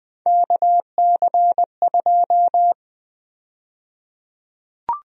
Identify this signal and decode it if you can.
Morse code, heard as a keyed tone, about 21 wpm, 700 Hz. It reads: KC2
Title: USA Call Sign Prefixes - 20wpm